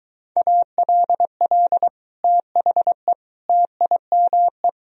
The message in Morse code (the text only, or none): ALL THE TIME